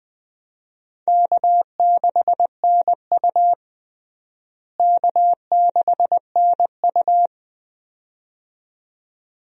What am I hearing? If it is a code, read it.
Morse code, heard as a keyed tone, about 20 wpm, 700 Hz. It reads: K6NU K6NU
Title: USA Call Signs - 4 Characters - Repeated 2x - 20wpm